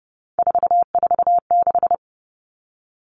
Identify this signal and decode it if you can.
Morse code, heard as a keyed tone, about 30 wpm, 700 Hz. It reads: 446
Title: Numbers - 3-Digit 30wpm